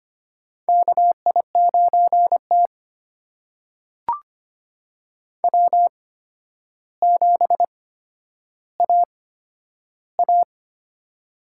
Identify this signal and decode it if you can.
Morse code, heard as a keyed tone, about 25 wpm, spaced as for 11 wpm, 700 Hz. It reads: KI9T W7AA
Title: USA Call Signs - 4 Characters - 8x Character Spacing - 25wpm